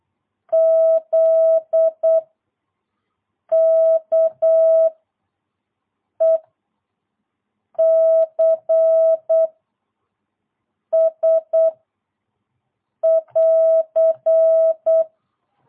Morse code, heard as a keyed tone, about 8 wpm, 660 Hz. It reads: Z K E C S <AR>
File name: jh8cbh / 2022cw